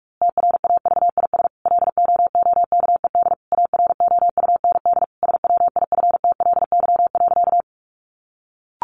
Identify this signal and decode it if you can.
Morse code, heard as a keyed tone, about 44 wpm, 700 Hz. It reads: TRAVIS LOOKED AROUND SWIFTLY.
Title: A Sound Of Thunder 45wpm